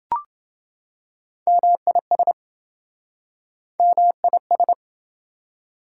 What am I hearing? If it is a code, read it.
Morse code, heard as a keyed tone, about 26 wpm, 700 Hz. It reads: MIS MIS